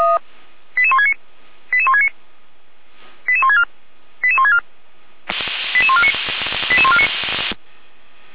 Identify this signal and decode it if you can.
Five-tone selective call (ZVEI), heard as quick runs of five tones, tones of about 70 ms. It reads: 88169 88169 88164 88164 88149 88149